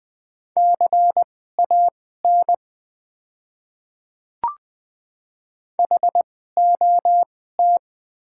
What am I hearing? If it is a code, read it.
Morse code, heard as a keyed tone, about 20 wpm, spaced as for 14 wpm, 690 Hz. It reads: CAN HOT
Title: 2x Character Spacing - Top 100 Words - 20wpm